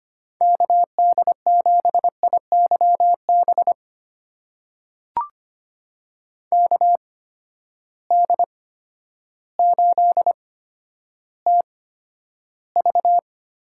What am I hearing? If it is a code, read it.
Morse code, heard as a keyed tone, about 25 wpm, spaced as for 11 wpm, 700 Hz. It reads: KD7IYB KD8TV
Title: USA Call Signs - 6 Characters - 8x Character Spacing - 25wpm